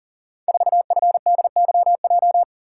Morse code, heard as a keyed tone, about 38 wpm, 700 Hz. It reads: 4FDYJ
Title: Code Groups - 5 Letter:Numbers - 40wpm